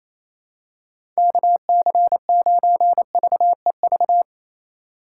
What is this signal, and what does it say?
Morse code, heard as a keyed tone, about 28 wpm, 700 Hz. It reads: KC9VEV
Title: USA Call Signs - 6 Characters - 2x Character Spacing - 28wpm